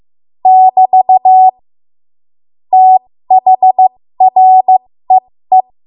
Morse code, heard as a keyed tone, about 15 wpm, spaced as for 13 wpm, 750 Hz. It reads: <BT> THREE
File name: Morse / 13WPM